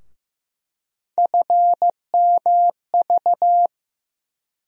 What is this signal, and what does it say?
Morse code, heard as a keyed tone, about 15 wpm, 700 Hz. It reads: FMV
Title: USA Call Sign Postfixes - 3 Letters 15wpm